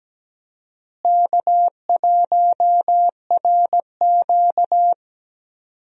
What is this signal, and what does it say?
Morse code, heard as a keyed tone, about 17 wpm, 700 Hz. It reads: K1RQ
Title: USA Call Signs - 4 Characters - Repeated 2x - 17wpm